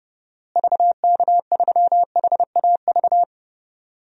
Morse code, heard as a keyed tone, about 30 wpm, 700 Hz. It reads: VK3HAV
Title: DXCC-100 Call Signs - 6 Characters - Repeated 2x - 30wpm